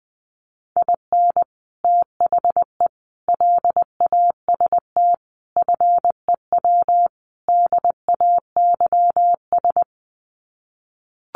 Morse code, heard as a keyed tone, about 20 wpm, 700 Hz. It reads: IN THE LAST FEW DAYS
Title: Sets of 5 Words Encore2 20wpm